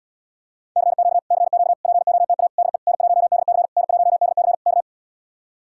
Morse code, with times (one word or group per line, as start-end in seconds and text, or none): 0.8-1.2 CQ
1.3-1.7 CQ
1.8-2.5 POTA
2.6-2.8 DE
2.9-3.7 N1DC
3.8-4.5 N1DC
4.7-4.8 K